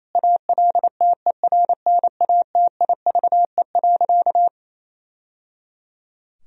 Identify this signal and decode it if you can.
Morse code, heard as a keyed tone, about 27 wpm, 700 Hz. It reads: ALTERNATIVE.
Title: Sentences from Top 1000 Words - Review Part 4 - 28wpm